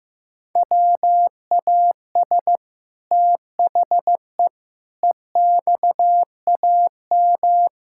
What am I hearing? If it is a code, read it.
Morse code, heard as a keyed tone, about 15 wpm, 700 Hz. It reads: WAS THE EXAM
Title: Sentences from Top 1000 Words - Review Part 5 - 15wpm